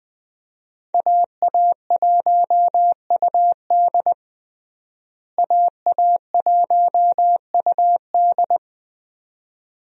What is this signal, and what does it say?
Morse code, heard as a keyed tone, about 21 wpm, 700 Hz. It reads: AA1UD AA1UD
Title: USA Call Signs - 5 Characters - Repeated 2x - 20wpm